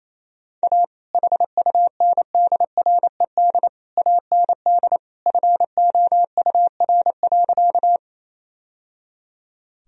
Morse code, heard as a keyed tone, about 28 wpm, 700 Hz. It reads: A HUNDRED AND FOUR.
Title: Sentences from Top 1000 Words - Review Part 2 - 28wpm